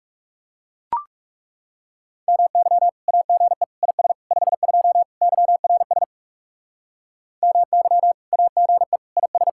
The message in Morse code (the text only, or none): MY AGE IS 52 YRS MY AGE IS